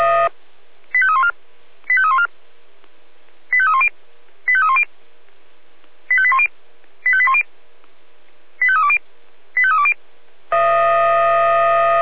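Five-tone selective call (ZVEI), heard as quick runs of five tones, tones of about 70 ms. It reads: 85314 85314 85310 85310 85810 85810 85320 85320
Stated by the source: ZVEI five-tone selective call received by SDR.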